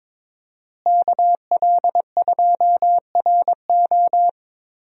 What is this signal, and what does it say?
Morse code, heard as a keyed tone, about 22 wpm, 700 Hz. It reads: KL2RO
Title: USA Call Signs - 5 Characters - Repeated 3x - 22wpm